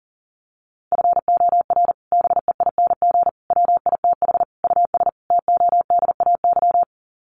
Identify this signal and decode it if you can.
Morse code, heard as a keyed tone, about 37 wpm, 700 Hz. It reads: FOR BEING WITH US TODAY
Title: Sets of 5 Words Encore2 40wpm